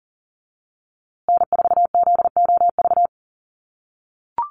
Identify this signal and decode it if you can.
Morse code, heard as a keyed tone, about 40 wpm, 700 Hz. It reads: N4ZOV
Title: USA Call Signs Repeated 3x 40wpm